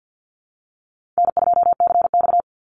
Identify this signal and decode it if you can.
Morse code, heard as a keyed tone, about 50 wpm, 700 Hz. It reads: N2CX
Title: Call Signs - 50wpm